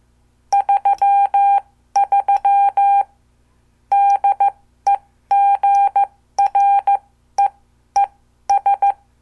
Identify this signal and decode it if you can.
Morse code, heard as a keyed tone, about 14 wpm, spaced as for 12 wpm, 800 Hz. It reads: 33 DEGREES
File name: morse / qso